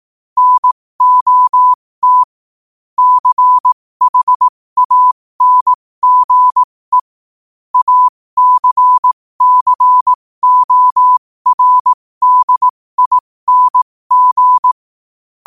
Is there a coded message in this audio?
Yes: Morse code from 0.4 to 14.7 s, heard as a keyed tone, about 18 wpm, spaced as for 15 wpm, 1000 Hz. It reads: NOT CHANGE ACCORDING